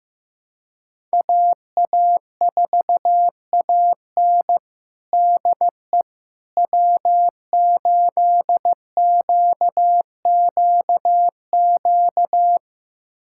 Morse code, heard as a keyed tone, about 15 wpm, 700 Hz. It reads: AA4AN DE W8QQQ
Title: USA Callsign DE Callsign 15wpm